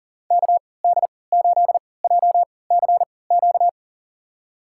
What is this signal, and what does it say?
Morse code, heard as a keyed tone, about 38 wpm, spaced as for 22 wpm, 700 Hz. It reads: KD8JCQ
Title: USA Call Signs - 6 Characters - 3x Character Spacing - 40wpm